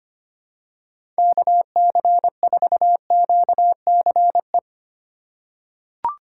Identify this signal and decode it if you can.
Morse code, heard as a keyed tone, about 25 wpm, 700 Hz. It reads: KC4QCE
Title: USA Call Signs - 6 Characters - 4x Character Spacing - 25wpm